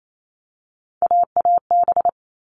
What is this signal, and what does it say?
Morse code, heard as a keyed tone, about 28 wpm, 700 Hz. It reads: AAB
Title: All Cut Numbers - 3-Digit 28wpm